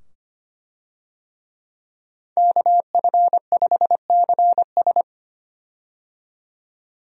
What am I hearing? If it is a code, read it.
Morse code, heard as a keyed tone, about 25 wpm, 700 Hz. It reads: KF5CS
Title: USA Call Signs - 5 Characters - 25wpm